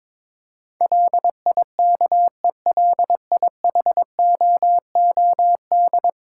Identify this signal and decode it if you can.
Morse code, heard as a keyed tone, about 22 wpm, 700 Hz. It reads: LIKELIHOOD
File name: practice / 22wpm / 10 Letter Words - First 3 Letters Given - 22wpm